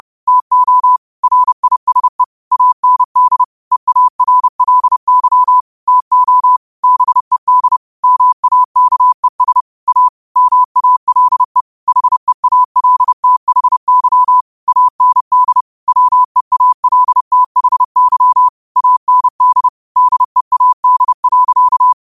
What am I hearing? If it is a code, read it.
Morse code, heard as a keyed tone, about 30 wpm, 1000 Hz. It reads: TO RISE AND EARLY TO BED MAKES A MALE HEALTHY AND WEALTHY AND DEAD.
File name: qotdfiles / 30wpm